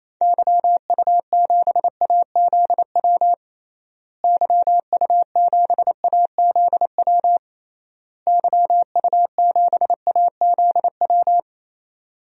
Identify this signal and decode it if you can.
Morse code, heard as a keyed tone, about 28 wpm, 700 Hz. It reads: YU7AZW YU7AZW YU7AZW
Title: DXCC-100 Call Signs - 6 Characters - Repeated 3x - 28wpm